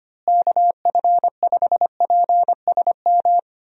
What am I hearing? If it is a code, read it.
Morse code, heard as a keyed tone, about 25 wpm, 700 Hz. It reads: KF5PSM
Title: USA Call Signs - 6 Characters - Repeated 2x - 25wpm